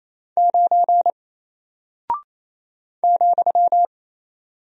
Morse code, heard as a keyed tone, about 27 wpm, 700 Hz. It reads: 9 ,
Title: Single Character (Letter:Number:Punctuation:Prosign) 28wpm